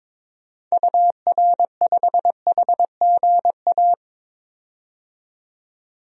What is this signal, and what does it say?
Morse code, heard as a keyed tone, about 21 wpm, 700 Hz. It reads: UR5HGA
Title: DXCC-100 Call Signs - 6 Characters - 22wpm